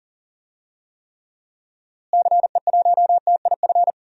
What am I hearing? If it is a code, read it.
Morse code, heard as a keyed tone, about 40 wpm, 700 Hz. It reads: CE1TIF
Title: DXCC-100 Call Signs - 6 Characters - Repeated 2x - 40wpm